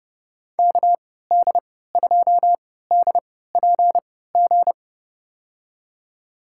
Morse code, heard as a keyed tone, about 32 wpm, spaced as for 17 wpm, 700 Hz. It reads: KD2DPG
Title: USA Call Signs - 6 Characters - 3x Character Spacing - 30wpm